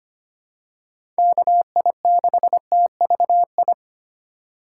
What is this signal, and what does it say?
Morse code, heard as a keyed tone, about 25 wpm, 690 Hz. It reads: KI6TVI
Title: USA Call Signs - 6 Characters - 2x Character Spacing - 25wpm